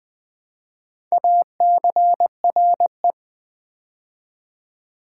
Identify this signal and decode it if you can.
Morse code, heard as a keyed tone, about 20 wpm, 700 Hz. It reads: ACRE
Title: Build A Word - 2 Letters At A Time - 4-Letter Words - 20wpm